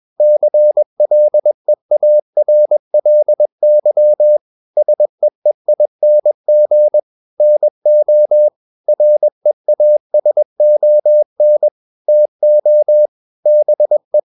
Morse code, heard as a keyed tone, about 21 wpm, 600 Hz. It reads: CLEARLY SEEING NO REASON TO BE
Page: CW med Gnister